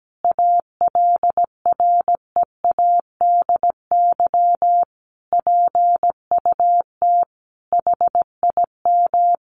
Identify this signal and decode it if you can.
Morse code, heard as a keyed tone, about 17 wpm, 700 Hz. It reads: ALREADY PUT HIM
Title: Sentences from Top 1200 Words 17wpm